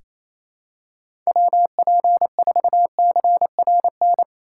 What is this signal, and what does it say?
Morse code, heard as a keyed tone, about 28 wpm, 700 Hz. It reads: WP4CRN
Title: USA Call Signs - 6 Characters - 2x Character Spacing - 28wpm